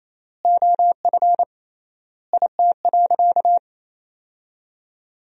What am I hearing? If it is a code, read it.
Morse code, heard as a keyed tone, about 28 wpm, 700 Hz. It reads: OF IT.
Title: 3x Word-Spacing - Sentences from Top 400 Words - 28wpm